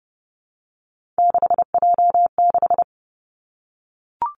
Morse code, heard as a keyed tone, about 30 wpm, 700 Hz. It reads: 6J6
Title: Three Challenging Letter-Number Combinations 30wpm